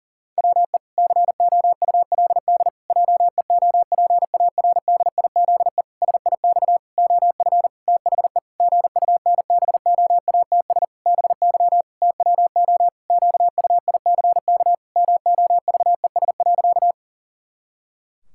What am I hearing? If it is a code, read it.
Morse code, heard as a keyed tone, about 40 wpm, 700 Hz. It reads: WE COULD JEOPARDIZE SIX OF THE GUNBOATS BY TWO QUICK MOVES.
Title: Pangram Challenge v2 - 40wpm